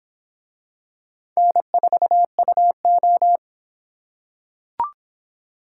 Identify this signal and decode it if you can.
Morse code, heard as a keyed tone, about 26 wpm, 700 Hz. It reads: N4UO